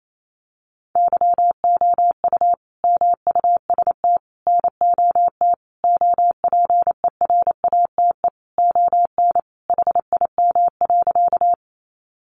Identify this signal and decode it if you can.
Morse code, heard as a keyed tone, about 28 wpm, 700 Hz. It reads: YOU MUST NOT OPERATE ON HIM.
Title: Sentences from Top 1500 Words 28wpm